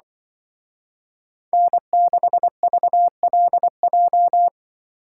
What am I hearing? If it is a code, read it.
Morse code, heard as a keyed tone, about 24 wpm, 700 Hz. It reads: N6VLJ
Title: Speed Racing - USA 1x3 Call Signs 20wpm